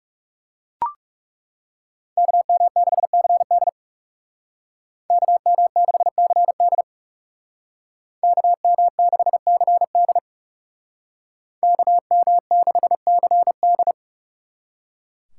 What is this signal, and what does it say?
Morse code, heard as a keyed tone, about 40 wpm, 700 Hz. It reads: KM6CD KM6CD KM6CD KM6CD